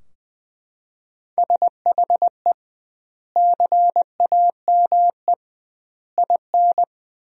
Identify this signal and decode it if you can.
Morse code, heard as a keyed tone, about 20 wpm, 700 Hz. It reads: SHE CAME IN
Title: 2x Word-Spacing - Sentences from Top 300 Words - 20wpm